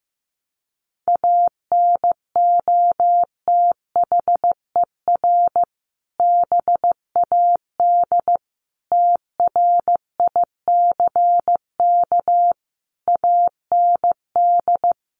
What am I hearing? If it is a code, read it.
Morse code, heard as a keyed tone, about 15 wpm, 700 Hz. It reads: ANOTHER BAD TRICK AND
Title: Sentences from Top 1500 Words 15wpm